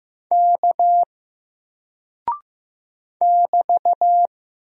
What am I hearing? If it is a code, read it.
Morse code, heard as a keyed tone, about 15 wpm, 700 Hz. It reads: K <BT>
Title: Single Character (Letter:Number:Punctuation:Prosign) 15wpm